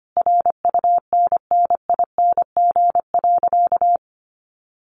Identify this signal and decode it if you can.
Morse code, heard as a keyed tone, about 25 wpm, 700 Hz. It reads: RUNNING.
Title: Additional Easy Phrases 25wpm